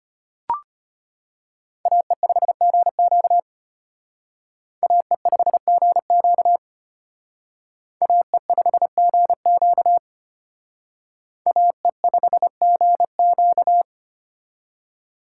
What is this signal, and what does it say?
Morse code, heard as a keyed tone, about 32 wpm, 700 Hz. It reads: AE5GQ AE5GQ AE5GQ AE5GQ